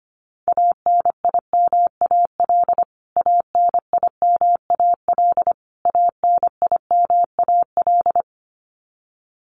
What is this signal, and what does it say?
Morse code, heard as a keyed tone, about 25 wpm, 700 Hz. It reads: ANIMAL ANIMAL ANIMAL
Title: Top 500 Words Repeated 25wpm